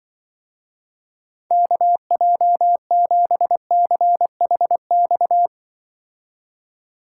Morse code, heard as a keyed tone, about 24 wpm, 700 Hz. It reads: KJ7CHX
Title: Speed Racing - USA 2x3 Call Signs 20wpm